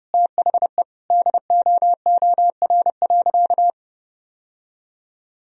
Morse code, heard as a keyed tone, about 30 wpm, 700 Hz. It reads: THE DOOR.
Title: Sentences from Top 1000 Words - Review Part 4 - 30wpm